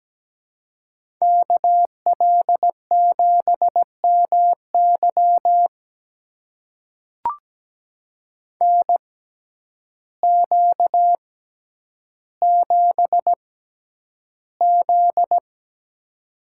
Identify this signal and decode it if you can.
Morse code, heard as a keyed tone, about 17 wpm, spaced as for 9 wpm, 700 Hz. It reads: KL7MY NQ7Z
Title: USA Call Signs - 6x Character Spacing - 17wpm